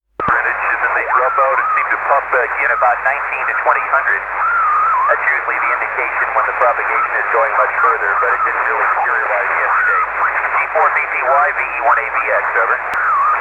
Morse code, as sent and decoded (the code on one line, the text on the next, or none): none